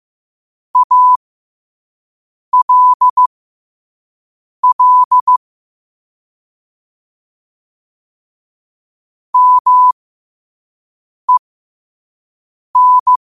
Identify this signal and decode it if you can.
Morse code, heard as a keyed tone, about 15 wpm, spaced as for 5 wpm, 1000 Hz. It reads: ALL MEN
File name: qotdfiles / 5wpm